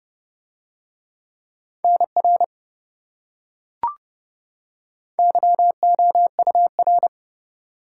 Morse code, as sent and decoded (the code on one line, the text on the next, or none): -. .-. / -.-- --- ..- .-.
NR YOUR